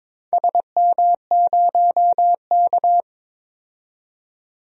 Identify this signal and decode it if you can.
Morse code, heard as a keyed tone, about 22 wpm, 700 Hz. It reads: SM0K
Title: DXCC-100 Call Signs - 4 Characters - 22wpm